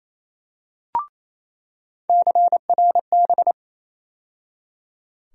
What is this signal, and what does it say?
Morse code, heard as a keyed tone, about 28 wpm, 700 Hz. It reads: CRB
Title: USA Call Sign Postfixes - 3 Letters 28wpm